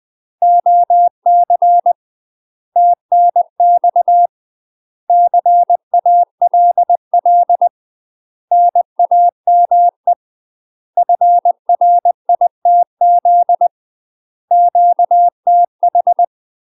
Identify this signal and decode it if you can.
Morse code, heard as a keyed tone, about 20 wpm, 700 Hz. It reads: OC TNX CALL NAME FRITZ QTH
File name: UK MASTER QSOs TRAINING / 20WPM / UK MASTER QSOs 0026